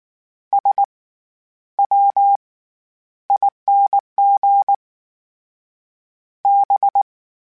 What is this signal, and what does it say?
Morse code, heard as a keyed tone, about 19 wpm, 800 Hz. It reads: S W ING B